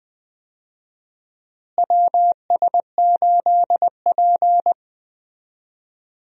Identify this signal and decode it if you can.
Morse code, heard as a keyed tone, about 20 wpm, 700 Hz. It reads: WS8P
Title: USA Call Signs - 4 Characters - Repeated 2x - 20wpm